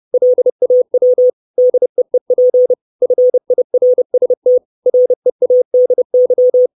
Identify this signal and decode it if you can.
Morse code, heard as a keyed tone, about 30 wpm, 500 Hz. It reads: LAW DEEP FIRST READY
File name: 632 Words 30WPM